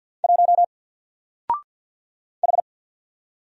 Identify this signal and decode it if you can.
Morse code, heard as a keyed tone, about 48 wpm, 700 Hz. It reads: 1 H
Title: Single Character (Letter:Number:Punctuation:Prosign) Alt 50wpm